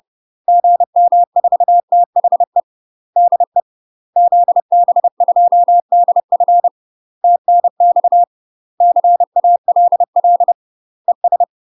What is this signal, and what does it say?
Morse code, heard as a keyed tone, about 30 wpm, 700 Hz. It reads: GM4THE DE ZB2DF TNX CALL ES
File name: UK MASTER QSOs TRAINING / 30WPM / UK MASTER QSOs 0000